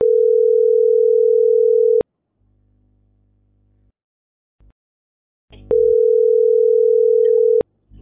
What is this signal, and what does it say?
Ringback tone (North America), 440 + 480 Hz.